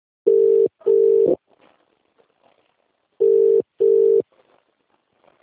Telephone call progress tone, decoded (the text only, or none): ringback tone